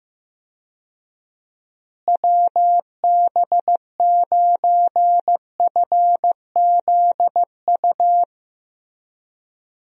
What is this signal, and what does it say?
Morse code, heard as a keyed tone, about 15 wpm, 700 Hz. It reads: WB9FZU
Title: USA Call Signs - 6 Characters - Repeated 2x - 15wpm